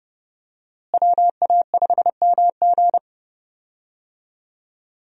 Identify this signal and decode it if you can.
Morse code, heard as a keyed tone, about 30 wpm, 700 Hz. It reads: WA5MG
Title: USA Call Signs - 5 Characters - Repeated 2x - 30wpm